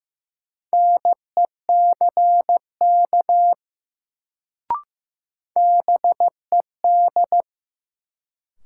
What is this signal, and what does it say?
Morse code, heard as a keyed tone, about 15 wpm, 700 Hz. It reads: NECK BED
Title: Sets of 5 Words Encore - New Word Drill - 15wpm